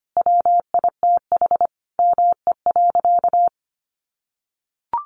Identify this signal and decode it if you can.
Morse code, heard as a keyed tone, about 25 wpm, 700 Hz. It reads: WITH ME.
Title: Sentences from Top 1400 Words 25wpm